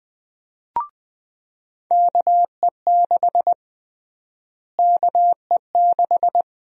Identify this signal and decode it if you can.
Morse code, heard as a keyed tone, about 20 wpm, 700 Hz. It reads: KE6 KE6
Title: USA Call Sign Prefixes - Repeated 2x - 20wpm